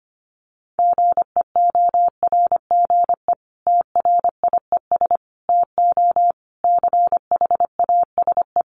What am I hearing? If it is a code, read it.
Morse code, heard as a keyed tone, about 25 wpm, 700 Hz. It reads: GEORGE TRIES TO CHASE
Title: Additional Easy Phrases 25wpm